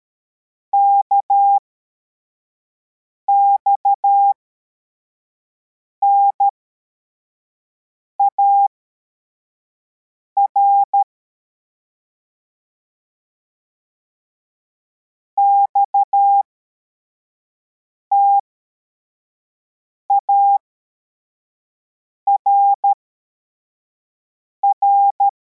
Morse code, heard as a keyed tone, about 13 wpm, spaced as for 4 wpm, 800 Hz. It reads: KXNAR XTARR
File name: formation / cw / Text23